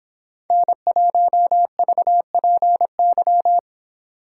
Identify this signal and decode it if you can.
Morse code, heard as a keyed tone, about 29 wpm, 700 Hz. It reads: N1VPY